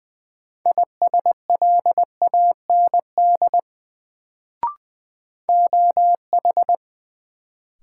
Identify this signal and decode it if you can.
Morse code, heard as a keyed tone, about 20 wpm, 700 Hz. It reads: ISLAND OH
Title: Sentences from Top 500 Words - New Word Drill - 20wpm